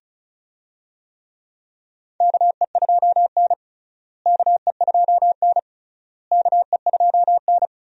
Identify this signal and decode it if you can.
Morse code, heard as a keyed tone, about 36 wpm, 700 Hz. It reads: KE2N KE2N KE2N
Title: USA Call Signs - 4 Characters - Repeated 3x - 35wpm